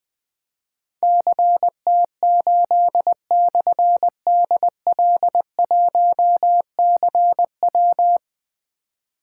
Morse code, heard as a keyed tone, about 20 wpm, 700 Hz. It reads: CT8/DL1CW
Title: Difficult Call Signs 20wpm